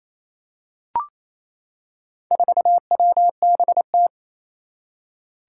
Morse code, heard as a keyed tone, about 28 wpm, 700 Hz. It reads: 4WBT
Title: Four Challenging Letter-Number Combinations 28wpm